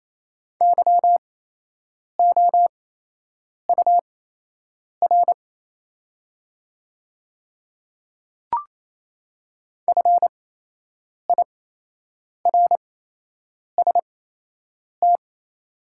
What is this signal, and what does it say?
Morse code, heard as a keyed tone, about 28 wpm, spaced as for 8 wpm, 700 Hz. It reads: YOUR FIRST
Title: 8x Character Spacing - Top 100 Words - 28wpm